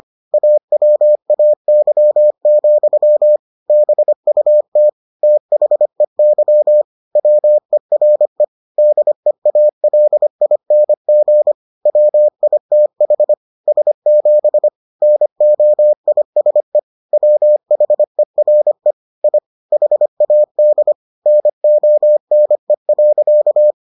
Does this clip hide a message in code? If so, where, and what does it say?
Yes: Morse code from 0.3 to 23.7 s, heard as a keyed tone, about 25 wpm, 600 Hz. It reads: AWAY, BUT THEY WERE DEALING WITH S7 NOISE WHERE I HAD NONE.